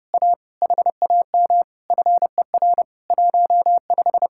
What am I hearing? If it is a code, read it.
Morse code, heard as a keyed tone, about 30 wpm, 700 Hz. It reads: A HAM FER 15